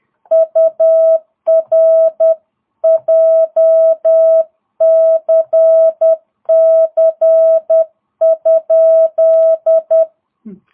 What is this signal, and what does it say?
Morse code, heard as a keyed tone, about 10 wpm, 650 Hz. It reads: UR JCC?